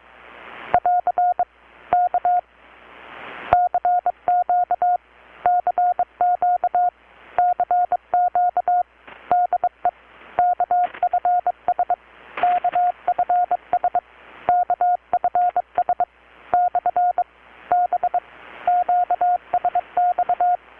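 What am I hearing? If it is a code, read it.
Morse code, heard as a keyed tone, about 22 wpm, 720 Hz. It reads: <AR> K CQ CQ CQ DE KFS KFS KFS / B QSX
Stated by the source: Morse from KPH's Night of Nights XX maritime radiotelegraphy event, received in Chiapas, Mexico about 3,800 km away on a Kenwood R-600 with a grounded 10 m long-wire antenna.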